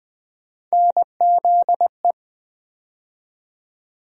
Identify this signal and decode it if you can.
Morse code, heard as a keyed tone, about 20 wpm, 700 Hz. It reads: NZE
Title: USA Call Sign Postfixes - 3 Letters 20wpm